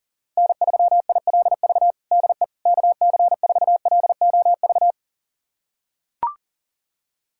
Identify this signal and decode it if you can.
Morse code, heard as a keyed tone, about 40 wpm, 700 Hz. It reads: N3ILV DE KC4LOV
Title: USA Callsign DE Callsign 40wpm